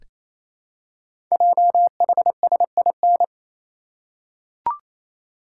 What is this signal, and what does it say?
Morse code, heard as a keyed tone, about 27 wpm, 700 Hz. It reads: JHSIN
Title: Code Groups - 5 Letters - 28wpm